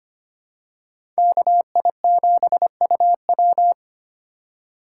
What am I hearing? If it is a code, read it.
Morse code, heard as a keyed tone, about 25 wpm, 700 Hz. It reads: KI7UW
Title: USA Call Signs - 5 Characters - 2x Character Spacing - 25wpm